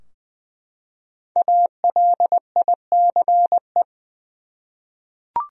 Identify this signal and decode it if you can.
Morse code, heard as a keyed tone, about 21 wpm, 700 Hz. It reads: ALICE
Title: USA Common Names - Max 5 Letters - 20wpm